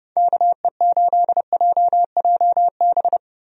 Morse code, heard as a keyed tone, about 30 wpm, 700 Hz. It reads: KE8JJB
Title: USA Call Signs - 6 Characters - 30wpm